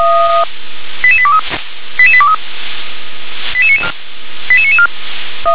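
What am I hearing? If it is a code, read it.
Five-tone selective call (ZVEI), heard as quick runs of five tones, tones of about 70 ms. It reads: 88823 88823 88994
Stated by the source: ZVEI five-tone call received on an SDR.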